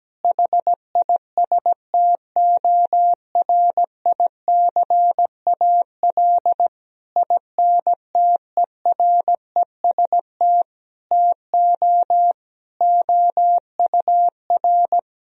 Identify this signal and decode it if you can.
Morse code, heard as a keyed tone, about 17 wpm, 700 Hz. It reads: HISTORICAL INTEREST TO OUR
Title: Sentences from Top 1000 Words - Review Part 4 - 17wpm